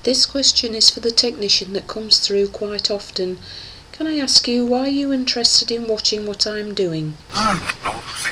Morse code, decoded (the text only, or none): none